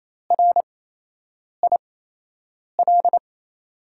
Morse code, heard as a keyed tone, about 27 wpm, spaced as for 8 wpm, 700 Hz. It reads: RIL
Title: USA Call Signs - 6 Characters - 8x Character Spacing - 28wpm